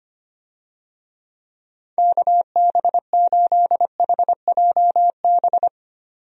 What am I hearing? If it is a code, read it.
Morse code, heard as a keyed tone, about 25 wpm, 700 Hz. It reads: KB8HJB
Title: USA Call Signs - 6 Characters - 25wpm